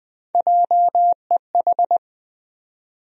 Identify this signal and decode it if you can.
Morse code, heard as a keyed tone, about 21 wpm, 700 Hz. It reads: JEH
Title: USA Call Sign Postfixes - 3 Letters 20wpm